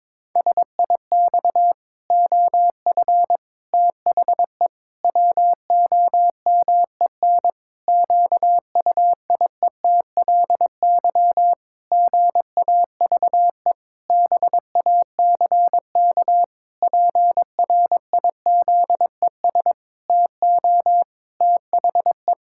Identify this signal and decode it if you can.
Morse code, heard as a keyed tone, about 22 wpm, 700 Hz. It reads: SIX OF THE WOMEN QUIETLY GAVE BACK PRIZES TO THE
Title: Pangram Challenge v2 - 22wpm